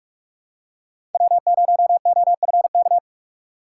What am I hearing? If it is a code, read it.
Morse code, heard as a keyed tone, about 46 wpm, 700 Hz. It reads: W0OFK
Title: USA Call Signs - 5 Characters - 2x Character Spacing - 45wpm